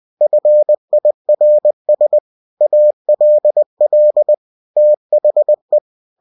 Morse code, heard as a keyed tone, about 20 wpm, 600 Hz. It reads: FIRS ALL THE
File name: cw-books / winnie-the-pooh / winnie-20wpm-page-0028